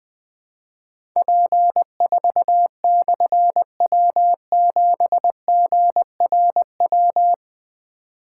Morse code, heard as a keyed tone, about 20 wpm, 700 Hz. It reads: P4/W7GRW
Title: Difficult Call Signs 20wpm